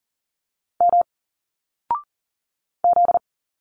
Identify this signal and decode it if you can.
Morse code, heard as a keyed tone, about 40 wpm, 700 Hz. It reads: M Z
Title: Single Letters 40wpm